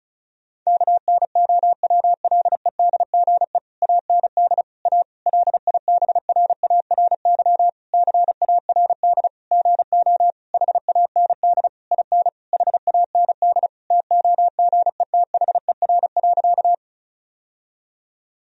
Morse code, heard as a keyed tone, about 36 wpm, 700 Hz. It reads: KNOWLEDGE AND A LIBRARY CARD GO HAND IN HAND TOGETHER.
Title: Sentences from Top 1000 Words - Review Part 4 - 35wpm